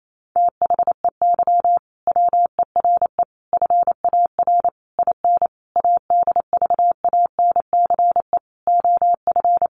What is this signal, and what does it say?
Morse code, heard as a keyed tone, about 28 wpm, 700 Hz. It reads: THEY WERE FAR IN ADVANCE OF
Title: Sentences from Top 1400 Words 28wpm